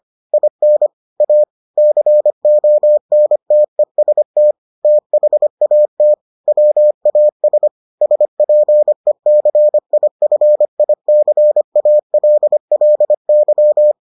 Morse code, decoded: IN A CONTEST THAT WAS SPECIFICALLY

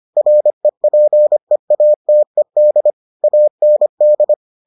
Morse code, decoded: REPEATED AND

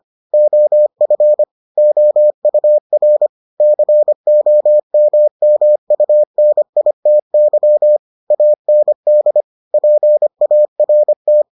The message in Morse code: OF OUR COMMUNITY AND PART

